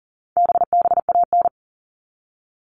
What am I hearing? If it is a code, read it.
Morse code, heard as a keyed tone, about 40 wpm, 700 Hz. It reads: BBAN
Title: All Cut Numbers - 4-Digit 40wpm